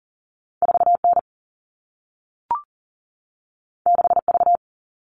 Morse code, heard as a keyed tone, about 42 wpm, 700 Hz. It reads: 4N 6V
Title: All Cut Numbers - 2-Digit 40wpm